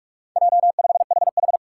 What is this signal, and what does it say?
Morse code, heard as a keyed tone, about 46 wpm, 700 Hz. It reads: J5HH
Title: Four Challenging Letter-Number Combinations 45wpm